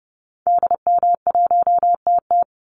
Morse code, heard as a keyed tone, about 28 wpm, 700 Hz. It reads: DM1TT
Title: Call Signs Repeated 3x - 30wpm